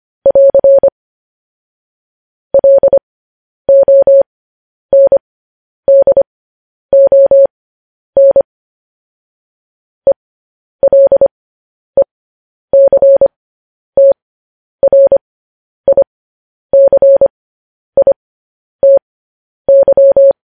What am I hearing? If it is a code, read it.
Morse code, heard as a keyed tone, about 25 wpm, spaced as for 10 wpm, 550 Hz. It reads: <AR> LONDON ELECTRICITY
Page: news headlines in morse code at 10 wpm